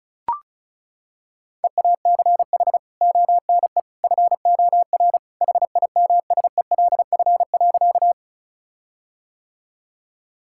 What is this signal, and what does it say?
Morse code, heard as a keyed tone, about 34 wpm, 700 Hz. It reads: EACH ONE FOR HIMSELF.
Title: Sentences from Top 1600 Words 35wpm